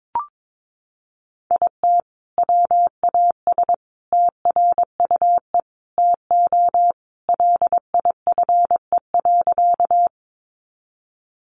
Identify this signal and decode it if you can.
Morse code, heard as a keyed tone, about 22 wpm, 700 Hz. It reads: IT WAS TRUE TO LIFE.
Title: Sentences from Top 400 Words 22wpm